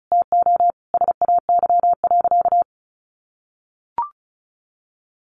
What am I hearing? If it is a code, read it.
Morse code, heard as a keyed tone, about 34 wpm, 700 Hz. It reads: TO SAY.
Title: Sentences from Top 500 Words 35wpm